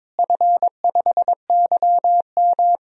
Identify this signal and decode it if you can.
Morse code, heard as a keyed tone, about 22 wpm, 700 Hz. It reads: F5YM
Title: DXCC-100 Call Signs - 4 Characters - Repeated 2x - 22wpm